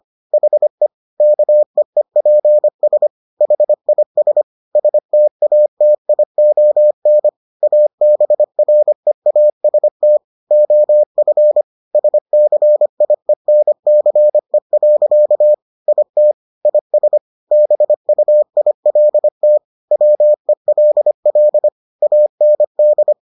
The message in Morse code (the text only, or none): HE KEEPS HIS STATION ABREAST OF SCIENCE. IT IS BUILT WELL AND